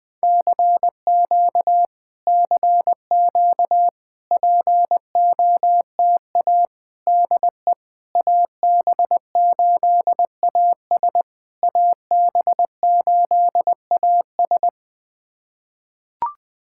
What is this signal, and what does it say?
Morse code, heard as a keyed tone, about 20 wpm, 690 Hz. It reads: CQ CQ POTA DE AB8AS AB8AS
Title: POTA QSO Practice - 20wpm